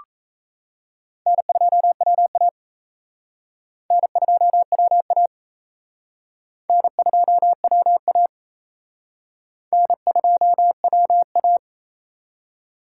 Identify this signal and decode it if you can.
Morse code, heard as a keyed tone, about 35 wpm, 700 Hz. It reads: N2WA N2WA N2WA N2WA
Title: Speed Racing - USA 1x2 Call Signs 28wpm